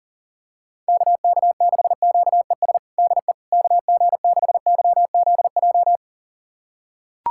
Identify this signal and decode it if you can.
Morse code, heard as a keyed tone, about 37 wpm, 700 Hz. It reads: KK6QES DE KG6YZJ
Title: USA Callsign DE Callsign - 6 Characters - 40wpm